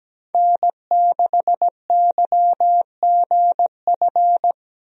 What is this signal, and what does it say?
Morse code, heard as a keyed tone, about 17 wpm, 700 Hz. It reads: N6YGF
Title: USA Call Signs - 5 Characters - Repeated 3x - 17wpm